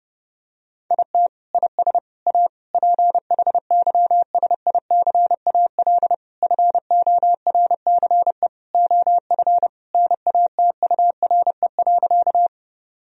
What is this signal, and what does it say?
Morse code, heard as a keyed tone, about 30 wpm, 700 Hz. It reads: IT IS A PHYSICAL FORCE OF NATURE.
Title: Sentences from Top 1000 Words - Review Part 4 - 30wpm